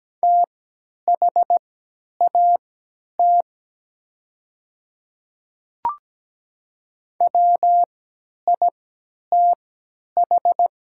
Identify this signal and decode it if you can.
Morse code, heard as a keyed tone, about 17 wpm, spaced as for 10 wpm, 690 Hz. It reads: THAT WITH
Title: 3x Character Spacing - Top 100 Words - 17wpm